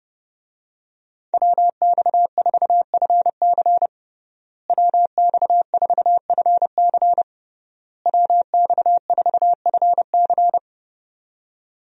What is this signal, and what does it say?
Morse code, heard as a keyed tone, about 30 wpm, 700 Hz. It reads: WX4FC WX4FC WX4FC
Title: USA Call Signs - 5 Characters - Repeated 3x - 30wpm